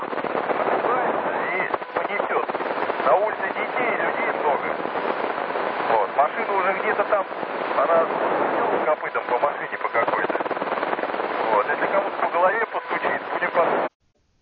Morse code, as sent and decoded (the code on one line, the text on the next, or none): none